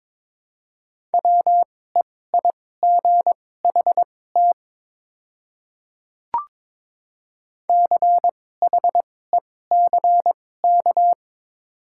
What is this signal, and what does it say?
Morse code, heard as a keyed tone, about 22 wpm, spaced as for 16 wpm, 700 Hz. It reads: WEIGHT CHECK
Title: 2x Character Spacing - Top 500 Words - 22wpm